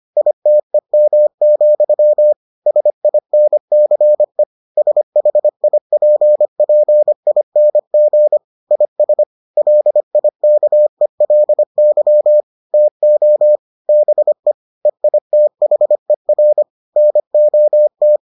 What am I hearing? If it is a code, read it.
Morse code, heard as a keyed tone, about 25 wpm, 600 Hz. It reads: ITEM, SINCE SHIPPING IS LIKELY TO BE EITHER NOT